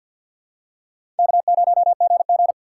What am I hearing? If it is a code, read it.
Morse code, heard as a keyed tone, about 47 wpm, 700 Hz. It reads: K0GG
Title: USA Call Signs - 4 Characters - 6x Character Spacing - 50wpm